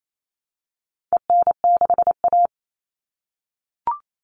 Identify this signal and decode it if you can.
Morse code, heard as a keyed tone, about 28 wpm, 700 Hz. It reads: EN6A
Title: All Cut Numbers - 4-Digit 28wpm